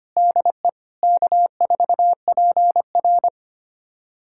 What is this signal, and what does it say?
Morse code, heard as a keyed tone, about 25 wpm, 700 Hz. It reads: DE K4PR
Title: USA Callsign DE Callsign - 4 Characters - 25wpm